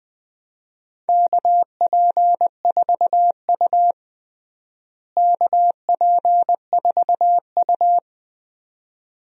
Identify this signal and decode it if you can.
Morse code, heard as a keyed tone, about 20 wpm, 700 Hz. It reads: KP4U KP4U
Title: DXCC-100 Call Signs - 4 Characters - Repeated 2x - 20wpm